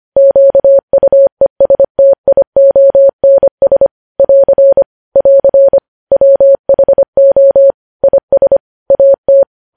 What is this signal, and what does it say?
Morse code, heard as a keyed tone, about 25 wpm, 550 Hz. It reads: QUESTIONS <AR> <AR> WHO IS AT
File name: podcast / News headlines-1x-25x25